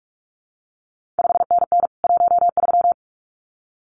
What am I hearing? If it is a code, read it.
Morse code, heard as a keyed tone, about 46 wpm, 700 Hz. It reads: 5NN 13
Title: CQ WW DX Contest Exchange Encore - 45wpm